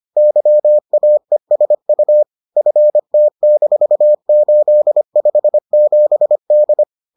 Morse code, heard as a keyed tone, about 25 wpm, 600 Hz. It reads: YAESU FT-857D